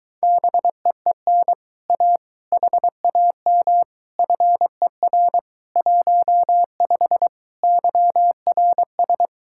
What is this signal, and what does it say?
Morse code, heard as a keyed tone, about 22 wpm, 700 Hz. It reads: BEEN A HAM FER 15 YRS